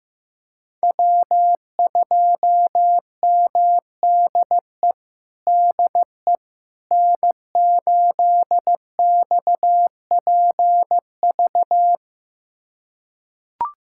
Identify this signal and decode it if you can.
Morse code, heard as a keyed tone, about 15 wpm, 700 Hz. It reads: W2MDE DE N8XPV
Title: USA Callsign DE Callsign - 5 Characters - 15wpm